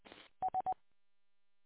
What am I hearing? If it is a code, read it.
There is no code.